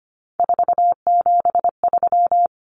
Morse code, heard as a keyed tone, about 25 wpm, 700 Hz. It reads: 473
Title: Numbers - 3-Digit 25wpm